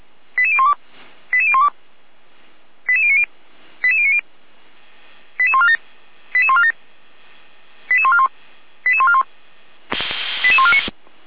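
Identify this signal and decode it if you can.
Five-tone selective call (ZVEI), heard as quick runs of five tones, tones of about 70 ms. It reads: 88012 88012 88090 88090 88147 88147 88141 88141 88149
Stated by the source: ZVEI five-tone selective call received with an SDR, recorded by the Feuerwehr Orlamünde (fire brigade) server.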